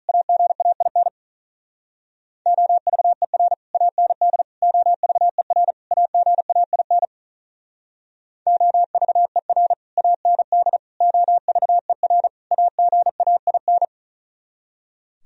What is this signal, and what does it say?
Morse code, heard as a keyed tone, about 46 wpm, 700 Hz. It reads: AGAIN OVER AND OVER AGAIN OVER AND OVER AGAIN